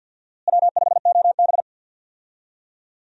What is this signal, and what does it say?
Morse code, heard as a keyed tone, about 47 wpm, 700 Hz. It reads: W5OB
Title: USA Call Signs - 4 Characters - Repeated 2x - 50wpm